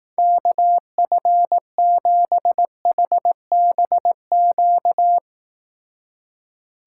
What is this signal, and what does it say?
Morse code, heard as a keyed tone, about 18 wpm, 700 Hz. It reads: KF7HBQ
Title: Speed Racing - USA 2x3 Call Signs 15wpm